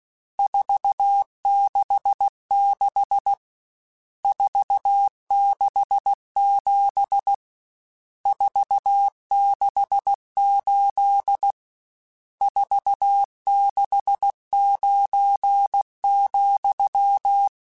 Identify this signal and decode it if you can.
Morse code, heard as a keyed tone, about 16 wpm, 800 Hz. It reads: 466 467 468 469,
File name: RSTs / RST-mp3s / Various RST @15